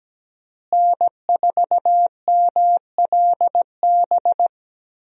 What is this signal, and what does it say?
Morse code, heard as a keyed tone, about 17 wpm, 700 Hz. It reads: N4MLB
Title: USA Call Signs - 5 Characters - Repeated 3x - 17wpm